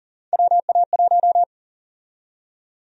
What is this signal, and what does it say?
Morse code, heard as a keyed tone, about 40 wpm, 700 Hz. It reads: WA1
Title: USA Call Sign Prefixes - Repeated 2x - 40wpm